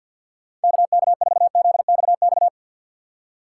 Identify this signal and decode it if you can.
Morse code, heard as a keyed tone, about 54 wpm, 700 Hz. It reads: KK4ZXX
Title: USA Call Signs - 6 Characters - 4x Character Spacing - 50wpm